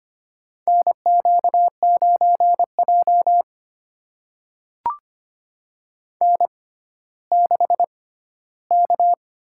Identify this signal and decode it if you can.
Morse code, heard as a keyed tone, about 25 wpm, spaced as for 14 wpm, 700 Hz. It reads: NQ9J N6K
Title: USA Call Signs - 6x Character Spacing - 25wpm